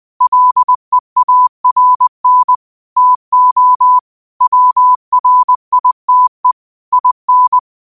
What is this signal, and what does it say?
Morse code, heard as a keyed tone, about 20 wpm, 1000 Hz. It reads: LEARN TO WRITE IN
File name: qotdfiles / 20wpm